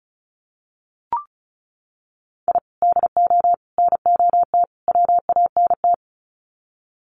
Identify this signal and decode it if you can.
Morse code, heard as a keyed tone, about 34 wpm, 700 Hz. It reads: I DO NOT WANT
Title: Sets of 4 Words 35wpm